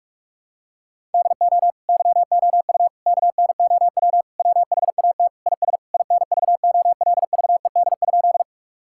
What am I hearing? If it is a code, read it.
Morse code, heard as a keyed tone, about 46 wpm, 700 Hz. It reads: DO YOU KNOW WHAT IS INVOLVED?
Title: Sentences from Top 1000 Words - Review Part 5 - 45wpm